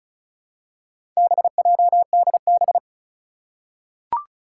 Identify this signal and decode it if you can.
Morse code, heard as a keyed tone, about 36 wpm, 700 Hz. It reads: BJDB
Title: Four Challenging Letter-Number Combinations 35wpm